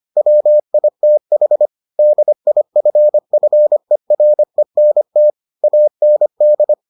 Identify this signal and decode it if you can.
Morse code, heard as a keyed tone, about 25 wpm, 600 Hz. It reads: WITH DIFFERENT AND